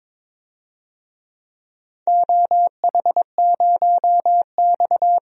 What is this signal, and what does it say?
Morse code, heard as a keyed tone, about 21 wpm, 700 Hz. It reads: OH0X
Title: DXCC-100 Call Signs - 4 Characters - 22wpm